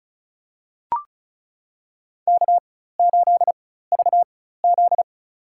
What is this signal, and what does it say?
Morse code, heard as a keyed tone, about 34 wpm, spaced as for 16 wpm, 700 Hz. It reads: K8VZ
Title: USA Call Signs - 4x Character Spacing - 35wpm